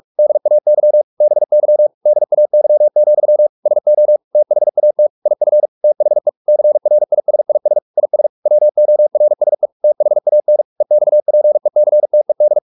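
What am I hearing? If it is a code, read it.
Morse code, heard as a keyed tone, about 44 wpm, 600 Hz. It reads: DAY BY DAY, SO THAT IF THE CRISIS IS WORSE THAN EXPECTED
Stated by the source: Morse practice at 45 wpm, the novel Never by Ken Follett in English, from CW med Gnister.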